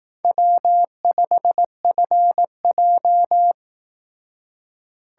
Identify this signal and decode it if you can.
Morse code, heard as a keyed tone, about 18 wpm, 700 Hz. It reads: W5FJ
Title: Speed Racing - USA 1x2 Call Signs 15wpm